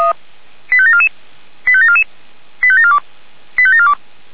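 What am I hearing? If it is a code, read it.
Five-tone selective call (ZVEI), heard as quick runs of five tones, tones of about 70 ms. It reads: 85744 85744 85742 85742
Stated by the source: Received on an SDR; ZVEI five-tone selective call.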